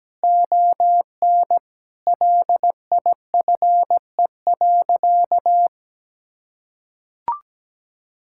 Morse code, heard as a keyed tone, about 17 wpm, 700 Hz. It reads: ON LIFE.
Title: Sentences from Top 1600 Words 17wpm